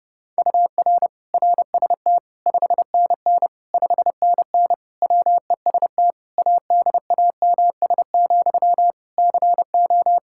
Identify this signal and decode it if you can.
Morse code, heard as a keyed tone, about 30 wpm, 700 Hz. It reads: UR RST 5NN 5NN WEST ADAMS, CO
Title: K1USN SST QSO Practice 30wpm